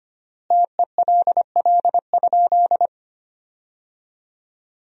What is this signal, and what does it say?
Morse code, heard as a keyed tone, about 25 wpm, 700 Hz. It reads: TELL?
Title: 3x Word-Spacing - Sentences from Top 200 Words - 25wpm